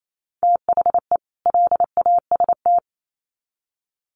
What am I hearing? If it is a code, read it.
Morse code, heard as a keyed tone, about 28 wpm, 700 Hz. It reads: THE LAST
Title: Sets of 2 Words 28wpm